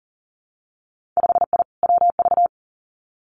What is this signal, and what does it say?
Morse code, heard as a keyed tone, about 40 wpm, 700 Hz. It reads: 5I WV
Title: Winter Field Day Exchanges Encore 40wpm